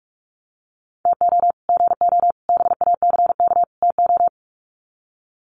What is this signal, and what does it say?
Morse code, heard as a keyed tone, about 44 wpm, 700 Hz. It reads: TO GO BACK TO
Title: Sets of 4 Words 45wpm